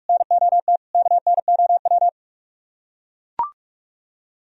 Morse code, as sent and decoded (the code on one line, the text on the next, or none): -. --- - / -.- -. --- .--
NOT KNOW